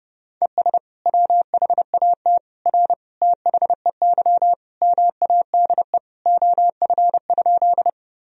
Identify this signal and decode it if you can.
Morse code, heard as a keyed tone, about 32 wpm, 700 Hz. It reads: ES WHAT R THEY MADE OF?
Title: Sentences from Top 200 Words and Abbreviations 30wpm